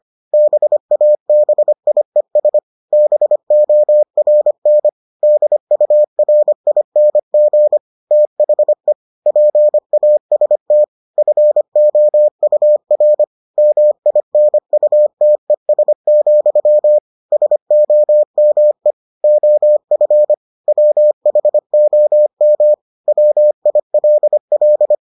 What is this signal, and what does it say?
Morse code, heard as a keyed tone, about 25 wpm, 600 Hz. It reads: BABIES BORN DURING THE PAST FOUR MINUTES, SOME OF WHOM WILL